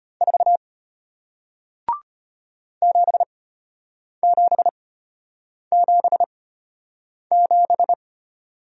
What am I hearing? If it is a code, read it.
Morse code, heard as a keyed tone, about 33 wpm, spaced as for 8 wpm, 700 Hz. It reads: 4 7777